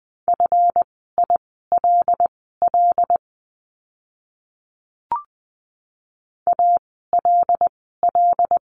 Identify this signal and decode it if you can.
Morse code, heard as a keyed tone, about 20 wpm, spaced as for 14 wpm, 700 Hz. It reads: FILL ALL
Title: 139 Lesson 23f - Letter f - Words 20wpm